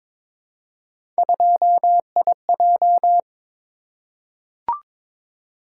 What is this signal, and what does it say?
Morse code, heard as a keyed tone, about 22 wpm, 700 Hz. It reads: 2IJ
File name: practice / 22wpm / Three Letter-Number Combinations - 22wpm